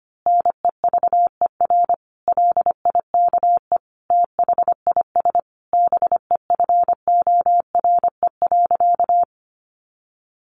Morse code, heard as a keyed tone, about 25 wpm, 700 Hz. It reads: NEVER LIKE THIS BEFORE.
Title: A Sound Of Thunder 25wpm